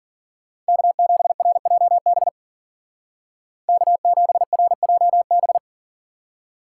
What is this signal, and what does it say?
Morse code, heard as a keyed tone, about 52 wpm, 700 Hz. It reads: K7RJB K7RJB